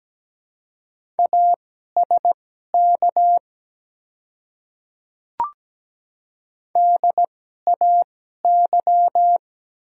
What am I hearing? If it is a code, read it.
Morse code, heard as a keyed tone, about 17 wpm, spaced as for 12 wpm, 700 Hz. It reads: ASK DAY
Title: 2x Character Spacing - Top 300 Words - 17wpm